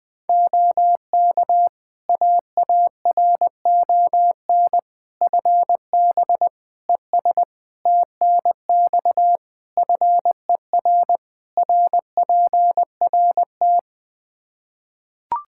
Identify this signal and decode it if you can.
Morse code, heard as a keyed tone, about 21 wpm, 700 Hz. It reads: OK AARON FB ES TNX FER RPRT
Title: K1USN SST QSO Practice 20wpm